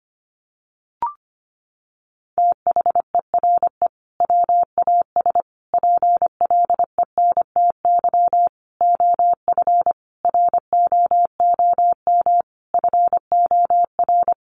Morse code, heard as a keyed tone, about 25 wpm, 700 Hz. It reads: THERE WAS PLENTY OF ROOM FOR
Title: Sentences from Top 1300 Words 25wpm